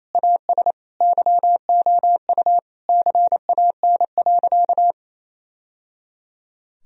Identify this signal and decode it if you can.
Morse code, heard as a keyed tone, about 27 wpm, 700 Hz. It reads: AS YOU CAN.
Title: Sentences from Top 1000 Words - Review Part 2 - 28wpm